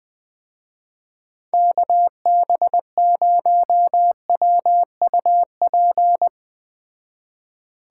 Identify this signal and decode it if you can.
Morse code, heard as a keyed tone, about 20 wpm, 700 Hz. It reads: KB0WUP